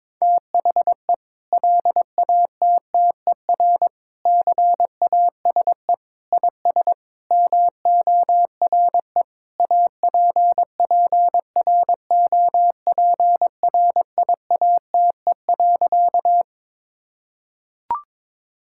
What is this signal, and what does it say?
Morse code, heard as a keyed tone, about 22 wpm, 700 Hz. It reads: THE LATTER CASE IS MORE APPROPRIATE.
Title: Sentences from Top 1600 Words 22wpm